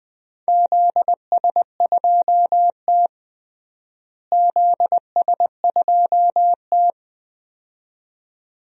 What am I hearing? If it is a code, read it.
Morse code, heard as a keyed tone, about 20 wpm, 700 Hz. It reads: ZS2T ZS2T
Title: DXCC-100 Call Signs - 4 Characters - Repeated 2x - 20wpm